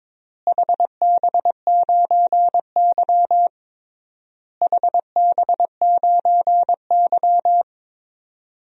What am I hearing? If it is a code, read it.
Morse code, heard as a keyed tone, about 22 wpm, 700 Hz. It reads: HB9Y HB9Y
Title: DXCC-100 Call Signs - 4 Characters - Repeated 3x - 22wpm